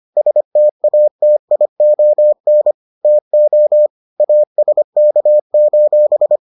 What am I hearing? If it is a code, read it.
Morse code, heard as a keyed tone, about 25 wpm, 600 Hz. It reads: STATION TO ASK: